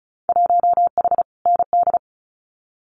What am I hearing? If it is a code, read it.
Morse code, heard as a keyed tone, about 34 wpm, 700 Hz. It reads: 1H ND
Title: Winter Field Day Exchanges 35wpm